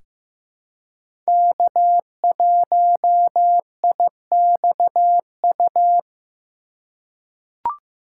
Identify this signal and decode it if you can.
Morse code, heard as a keyed tone, about 15 wpm, 700 Hz. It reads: K1IXU
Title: USA Call Signs - 5 Characters - 4x Character Spacing - 15wpm